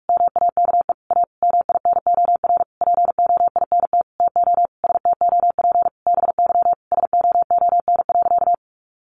Morse code, heard as a keyed tone, about 44 wpm, 700 Hz. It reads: MAKE A MINOR POINT TO STOP BY SOON.
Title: Sentences from Top 1500 Words 45wpm